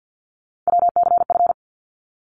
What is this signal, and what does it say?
Morse code, heard as a keyed tone, about 50 wpm, 700 Hz. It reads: WCF
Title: ARRL Sections 50wpm